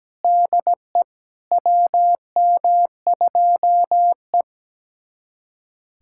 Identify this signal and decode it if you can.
Morse code, heard as a keyed tone, about 17 wpm, 700 Hz. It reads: DE WM2E
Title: USA Callsign DE Callsign - 4 Characters - 17wpm